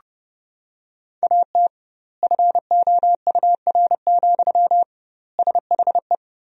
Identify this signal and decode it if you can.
Morse code, heard as a keyed tone, about 30 wpm, 700 Hz. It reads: AT FOUR, SHE
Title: 2x Word-Spacing - Sentences from Top 300 Words - 30wpm